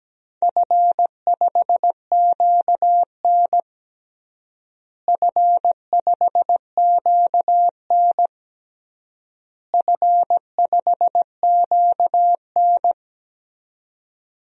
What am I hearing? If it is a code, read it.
Morse code, heard as a keyed tone, about 17 wpm, 700 Hz. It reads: F5QN F5QN F5QN
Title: DXCC-100 Call Signs - 4 Characters - Repeated 3x - 17wpm